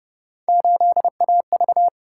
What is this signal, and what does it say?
Morse code, heard as a keyed tone, about 29 wpm, 700 Hz. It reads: 8AV